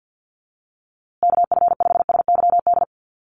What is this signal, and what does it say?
Morse code, heard as a keyed tone, about 50 wpm, 700 Hz. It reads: KF5SYD
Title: USA Call Signs Repeated 2x 50wpm